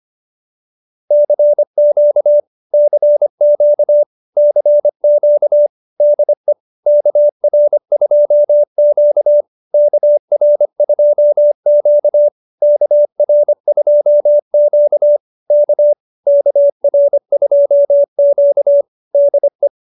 Morse code, heard as a keyed tone, about 25 wpm, 590 Hz. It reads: CQ CQ CQ DE KR2Q KR2Q KR2Q K KR2Q DE